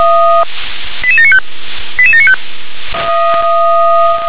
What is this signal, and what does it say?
Five-tone selective call (ZVEI), heard as quick runs of five tones, tones of about 80 ms. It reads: 88795 88795